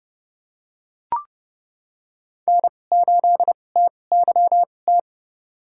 Morse code, heard as a keyed tone, about 30 wpm, spaced as for 22 wpm, 700 Hz. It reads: N8TYT
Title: USA Call Signs - 5 Characters - 2x Character Spacing - 30wpm